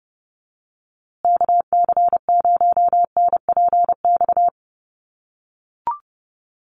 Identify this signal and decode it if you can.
Morse code, heard as a keyed tone, about 30 wpm, 700 Hz. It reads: KC0NPX
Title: USA Call Signs Repeated 2x 30wpm